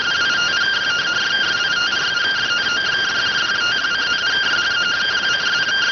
Two-tone teleprinter signal, about 170 Hz shift.